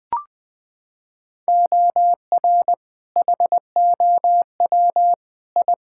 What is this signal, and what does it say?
Morse code, heard as a keyed tone, about 20 wpm, 700 Hz. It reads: OR HOW I
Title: Sets of 8 Words - 20wpm